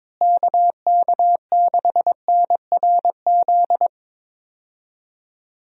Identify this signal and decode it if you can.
Morse code, heard as a keyed tone, about 22 wpm, 700 Hz. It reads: KK6NRZ
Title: USA Call Signs - 6 Characters - Repeated 2x - 22wpm